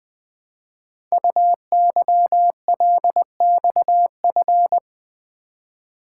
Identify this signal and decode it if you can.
Morse code, heard as a keyed tone, about 20 wpm, 700 Hz. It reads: UYLXF
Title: Code Groups - 5 Letters - 20wpm